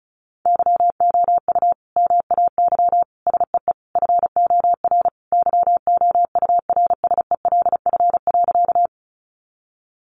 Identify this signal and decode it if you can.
Morse code, heard as a keyed tone, about 34 wpm, 700 Hz. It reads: YOU MAY SEE FOR YOURSELF.
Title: Sentences from Top 600 Words 35wpm